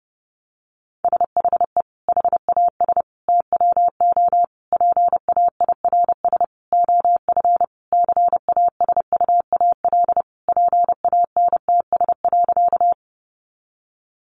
Morse code, heard as a keyed tone, about 30 wpm, 700 Hz. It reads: SHE HAS TWO PAIRS OF CASUAL PANTS.
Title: Additional Easy Phrases 30wpm